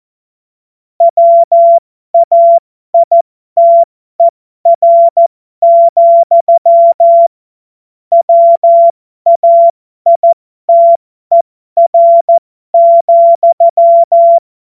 Morse code, heard as a keyed tone, about 14 wpm, spaced as for 12 wpm, 680 Hz. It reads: WAITER, WAITER,
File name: Morsekurs / Witze